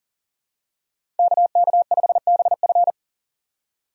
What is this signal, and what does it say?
Morse code, heard as a keyed tone, about 37 wpm, 700 Hz. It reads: KK5BF
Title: USA Call Signs - 5 Characters - 40wpm